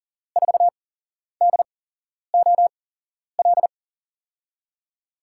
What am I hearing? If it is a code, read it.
Morse code, heard as a keyed tone, about 37 wpm, spaced as for 11 wpm, 700 Hz. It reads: 4DOL
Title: USA Call Signs - 6 Characters - 8x Character Spacing - 40wpm